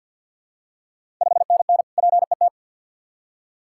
Morse code, heard as a keyed tone, about 48 wpm, 690 Hz. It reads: 5NN PET
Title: Ontario QSO Party Exchange 50wpm